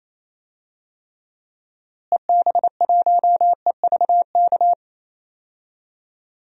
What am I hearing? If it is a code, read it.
Morse code, heard as a keyed tone, about 28 wpm, 700 Hz. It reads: EB1EVK
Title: DXCC-100 Call Signs - 6 Characters - Repeated 2x - 28wpm